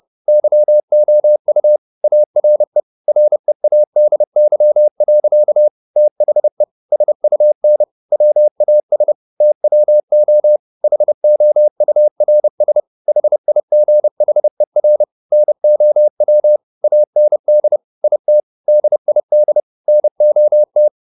Morse code, heard as a keyed tone, about 30 wpm, 600 Hz. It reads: YOU ARE READY. THE SUN WAS TWO HOURS HIGHER NOW AND IT DID NOT
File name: the old man and the sea / the old man and the sea 1 30wpm